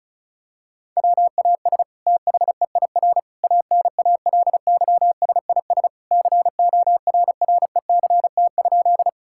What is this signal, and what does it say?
Morse code, heard as a keyed tone, about 34 wpm, 700 Hz. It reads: WAS THEIR ANALYSIS CORRECT?
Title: Sentences from Top 1000 Words - Review Part 5 - 35wpm